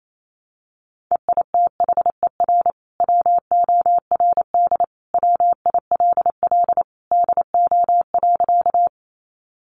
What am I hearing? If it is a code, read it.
Morse code, heard as a keyed tone, about 28 wpm, 700 Hz. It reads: EITHER WORD WILL DO.
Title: Sentences from Top 600 Words 28wpm